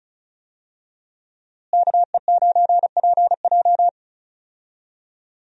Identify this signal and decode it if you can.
Morse code, heard as a keyed tone, about 34 wpm, 700 Hz. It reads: KE9PJ
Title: USA Call Signs - 5 Characters - Repeated 2x - 35wpm